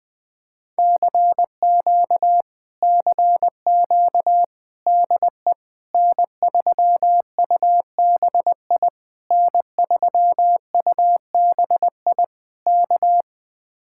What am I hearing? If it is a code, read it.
Morse code, heard as a keyed tone, about 19 wpm, 700 Hz. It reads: CQ CQ DE N3UBI N3UBI K
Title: Speed-Racing Common QSO Phrases - 20wpm